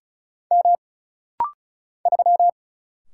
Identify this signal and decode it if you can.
Morse code, heard as a keyed tone, about 36 wpm, 700 Hz. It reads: M 3
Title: Single Letter-Number - Rapid-Fire 35wpm